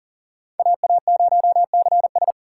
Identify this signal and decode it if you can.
Morse code, heard as a keyed tone, about 51 wpm, 700 Hz. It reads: AA0CS